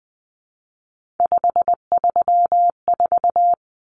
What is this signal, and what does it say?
Morse code, heard as a keyed tone, about 20 wpm, 700 Hz. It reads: 534
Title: Common Cut Numbers - 3-Digit 20wpm